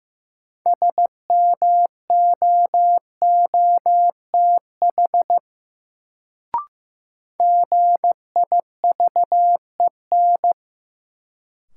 Morse code, heard as a keyed tone, about 15 wpm, 700 Hz. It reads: SMOOTH GIVEN
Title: Sentences from Top 1400 Words - New Word Drill - 15wpm